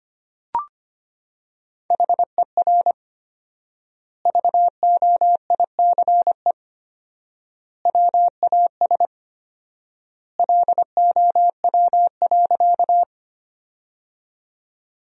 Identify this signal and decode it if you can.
Morse code, heard as a keyed tone, about 25 wpm, 700 Hz. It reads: HER VOICE WAS LOW.
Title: 4x Word-Spacing - Sentences from Top 500 Words - 25wpm